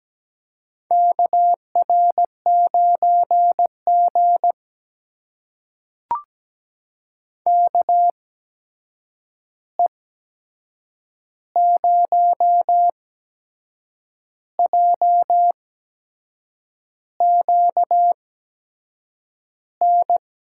Morse code, heard as a keyed tone, about 17 wpm, spaced as for 6 wpm, 700 Hz. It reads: KR9G KE0JQN
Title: USA Call Signs - 8x Character Spacing - 17wpm